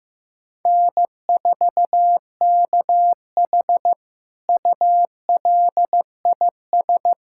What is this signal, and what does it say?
Morse code, heard as a keyed tone, about 16 wpm, 700 Hz. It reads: N4KH ULIS
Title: K1USN SST QSO Practice - 15wpm